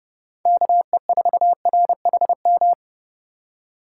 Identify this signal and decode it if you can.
Morse code, heard as a keyed tone, about 28 wpm, 700 Hz. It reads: KE4RHM